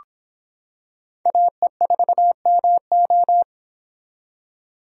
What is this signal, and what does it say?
Morse code, heard as a keyed tone, about 26 wpm, 700 Hz. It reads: AE4MO